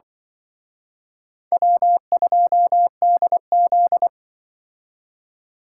Morse code, heard as a keyed tone, about 24 wpm, 700 Hz. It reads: W2DZ
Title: Speed Racing - USA 1x2 Call Signs 20wpm